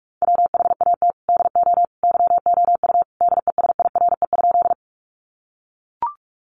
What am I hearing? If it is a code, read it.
Morse code, heard as a keyed tone, about 44 wpm, 700 Hz. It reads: WHAT DO YOU DESIRE?
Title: Sentences from Top 1100 Words 45wpm